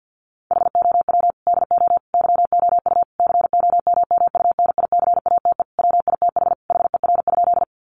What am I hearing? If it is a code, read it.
Morse code, heard as a keyed tone, about 50 wpm, 700 Hz. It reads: HOW DO YOU COMMUNICATE WITH HER?
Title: Sentences from Top 1300 Words 50wpm